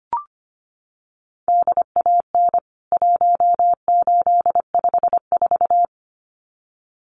Morse code, heard as a keyed tone, about 25 wpm, 700 Hz. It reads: DAN 1854
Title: CWT Exchange 25wpm